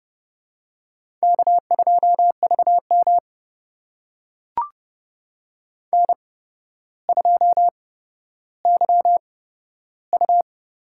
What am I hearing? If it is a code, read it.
Morse code, heard as a keyed tone, about 28 wpm, spaced as for 13 wpm, 700 Hz. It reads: K2VM N2YU
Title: USA Call Signs - 8x Character Spacing - 30wpm